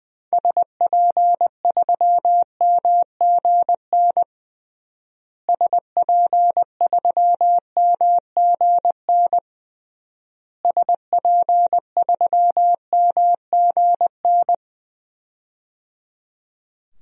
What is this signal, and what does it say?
Morse code, heard as a keyed tone, about 21 wpm, 700 Hz. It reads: SP3MGN SP3MGN SP3MGN
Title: DXCC-100 Call Signs - 6 Characters - Repeated 3x - 20wpm